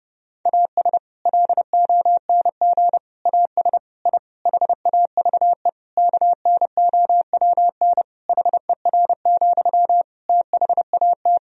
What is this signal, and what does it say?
Morse code, heard as a keyed tone, about 30 wpm, 700 Hz. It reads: AS LONG AS I HAVE KNOWN HER, THAT
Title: Sentences from Top 1000 Words - Review Part 3 - 30wpm